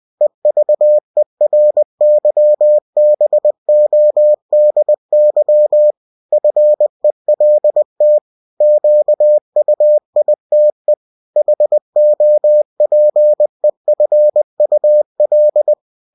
Morse code, heard as a keyed tone, about 20 wpm, 600 Hz. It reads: EVERYBODY FELT QUITE HOPEFUL
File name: cw-books / winnie-the-pooh / winnie-20wpm-page-0017